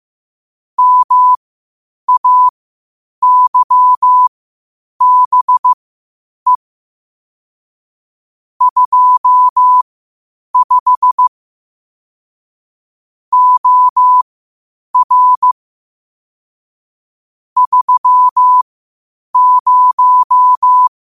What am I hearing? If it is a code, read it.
Morse code, heard as a keyed tone, about 15 wpm, spaced as for 8 wpm, 1000 Hz. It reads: MAYBE 25 OR 30
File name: qotdfiles / 8wpm